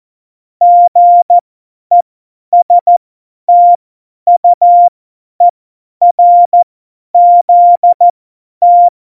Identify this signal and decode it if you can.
Morse code, heard as a keyed tone, about 14 wpm, spaced as for 10 wpm, 700 Hz. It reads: GESTUERZT